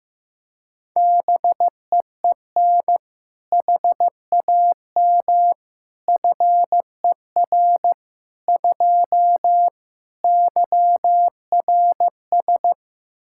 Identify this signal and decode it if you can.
Morse code, heard as a keyed tone, about 15 wpm, 700 Hz. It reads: BEEN HAM FER 2 YRS
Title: Practice QSO - Exchange 3 - More Info - 15wpm